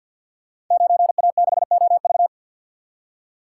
Morse code, heard as a keyed tone, about 51 wpm, 700 Hz. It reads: 9A6OV
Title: Code Groups - 5 Letter:Numbers - 50wpm